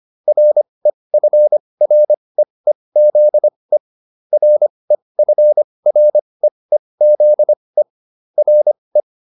Morse code, heard as a keyed tone, about 25 wpm, spaced as for 20 wpm, 600 Hz. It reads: REFREEZE REFREEZE RE